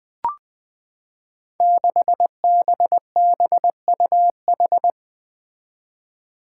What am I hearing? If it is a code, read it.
Morse code, heard as a keyed tone, about 20 wpm, 700 Hz. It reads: 6BBUH
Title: Five Challenging Letter-Number Combinations 20wpm